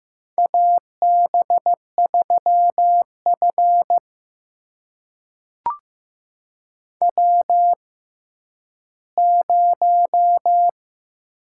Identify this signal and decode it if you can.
Morse code, heard as a keyed tone, about 15 wpm, spaced as for 10 wpm, 700 Hz. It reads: AB3F W0
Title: USA Call Signs - 4 Characters - 6x Character Spacing - 15wpm